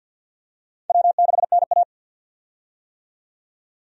Morse code, heard as a keyed tone, about 51 wpm, 700 Hz. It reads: W6NA
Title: USA Call Signs - 4 Characters - 50wpm